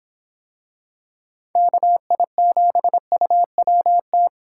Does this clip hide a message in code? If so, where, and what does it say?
Yes: Morse code from 1.6 to 4.3 s, heard as a keyed tone, about 25 wpm, 700 Hz. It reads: KI7UWT